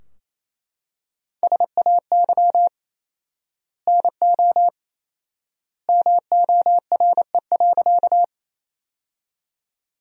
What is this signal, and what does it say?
Morse code, heard as a keyed tone, about 27 wpm, 700 Hz. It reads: SAY NO MORE.
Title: 4x Word-Spacing - Sentences from Top 200 Words - 28wpm